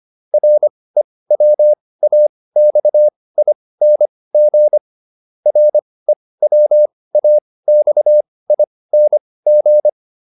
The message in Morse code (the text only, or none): REWAXING REWAXING